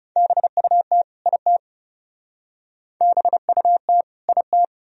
Morse code, heard as a keyed tone, about 37 wpm, 700 Hz. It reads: BUT IT BUT IT
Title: Speed-Racing - Sets of 2 Words 30wpm